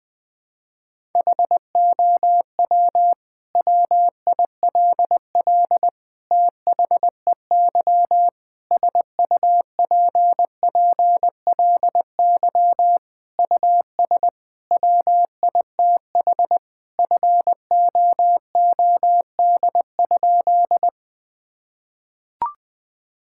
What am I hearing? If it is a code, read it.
Morse code, heard as a keyed tone, about 21 wpm, 700 Hz. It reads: HOW WILL THEY SUPPLY US WITH FOOD?
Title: Sentences from Top 1000 Words - Review Part 5 - 20wpm